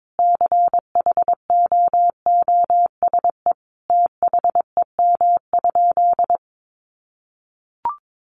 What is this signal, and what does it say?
Morse code, heard as a keyed tone, about 22 wpm, 700 Hz. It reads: CHOOSE THEM?
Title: Sentences from Top 700 Words 22wpm